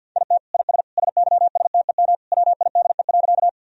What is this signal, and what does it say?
Morse code, heard as a keyed tone, about 52 wpm, 700 Hz. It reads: IT IS SYSTEM WIDE.
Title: Sentences from Top 1000 Words - Review Part 3 - 50wpm